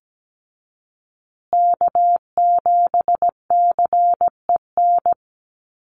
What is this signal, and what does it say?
Morse code, heard as a keyed tone, about 17 wpm, 700 Hz. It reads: K7CEN
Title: USA Call Signs Repeated 3x 17wpm